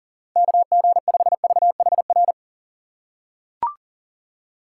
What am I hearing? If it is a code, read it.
Morse code, heard as a keyed tone, about 40 wpm, 700 Hz. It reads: KG5VHR
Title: USA Call Signs - 6 Characters - 6x Character Spacing - 40wpm